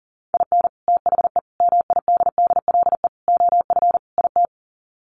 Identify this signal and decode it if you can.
Morse code, heard as a keyed tone, about 40 wpm, 700 Hz. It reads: IN THE MIDDLE OF IT
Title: Sets of 5 Words Encore 40wpm